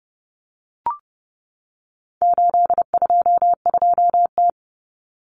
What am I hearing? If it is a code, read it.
Morse code, heard as a keyed tone, about 30 wpm, 700 Hz. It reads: 822T
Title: Common Cut Numbers - 4-Digit 30wpm